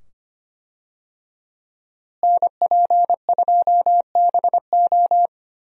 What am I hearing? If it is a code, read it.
Morse code, heard as a keyed tone, about 25 wpm, 700 Hz. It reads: NP2BO
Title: USA Call Signs - 5 Characters - Repeated 2x - 25wpm